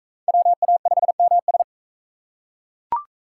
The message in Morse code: WA5MS

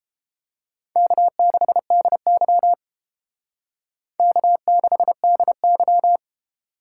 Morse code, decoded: K6DY K6DY

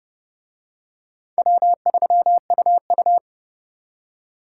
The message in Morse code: W3UU

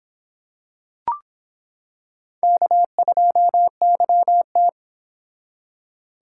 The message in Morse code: K2YT

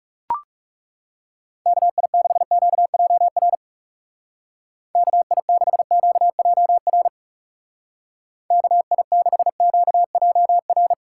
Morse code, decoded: KI6QJR KI6QJR KI6QJR

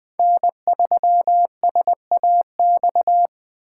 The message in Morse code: N3SAX

